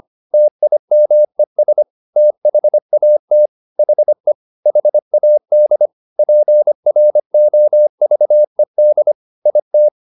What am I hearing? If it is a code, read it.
Morse code, heard as a keyed tone, about 25 wpm, 600 Hz. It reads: TIMES THAT HE HAD PROVED IT
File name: the old man and the sea part 1 25wpm